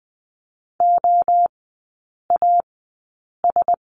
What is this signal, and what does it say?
Morse code, heard as a keyed tone, about 20 wpm, spaced as for 8 wpm, 700 Hz. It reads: OAS
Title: 100 Lesson 17e - Letter u - Three Characters 20wpm